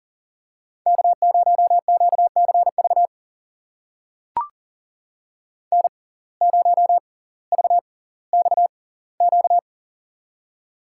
Morse code, heard as a keyed tone, about 40 wpm, spaced as for 20 wpm, 700 Hz. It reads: K0QCV N0VXQ
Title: USA Call Signs - 5 Characters - 6x Character Spacing - 40wpm